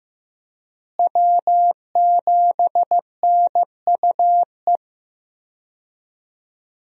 Morse code, heard as a keyed tone, about 15 wpm, 700 Hz. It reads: W7NUE
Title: USA Call Signs - 5 Characters - Repeated 2x - 15wpm